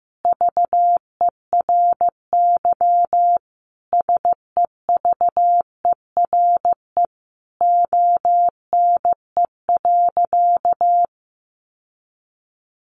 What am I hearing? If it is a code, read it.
Morse code, heard as a keyed tone, about 15 wpm, 700 Hz. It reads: VERY SEVERE ONE.
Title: Sentences from Top 1400 Words 15wpm